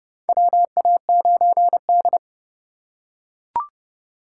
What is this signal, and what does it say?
Morse code, heard as a keyed tone, about 30 wpm, 700 Hz. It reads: WA9D
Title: USA Call Signs - 4 Characters - 3x Character Spacing - 30wpm